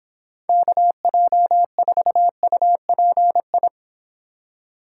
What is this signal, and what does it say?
Morse code, heard as a keyed tone, about 26 wpm, 700 Hz. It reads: KJ4UPI